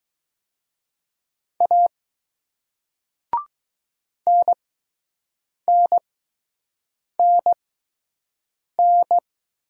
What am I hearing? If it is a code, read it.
Morse code, heard as a keyed tone, about 20 wpm, spaced as for 6 wpm, 700 Hz. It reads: A NNNN